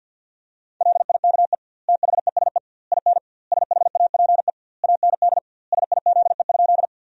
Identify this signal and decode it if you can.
Morse code, heard as a keyed tone, about 47 wpm, 700 Hz. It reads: LIKE THESE IN SHAPE AND SIZE?
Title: 2x Word-Spacing - Sentences from Top 500 Words - 50wpm